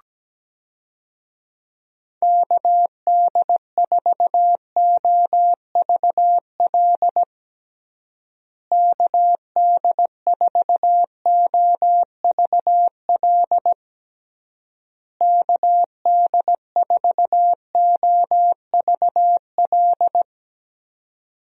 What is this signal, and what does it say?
Morse code, heard as a keyed tone, about 17 wpm, 700 Hz. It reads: KD4OVL KD4OVL KD4OVL
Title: USA Call Signs - 6 Characters - Repeated 3x - 17wpm